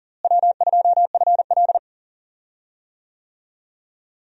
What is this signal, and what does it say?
Morse code, heard as a keyed tone, about 40 wpm, 700 Hz. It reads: W2FL
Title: USA Call Signs - 4 Characters - Repeated 2x - 40wpm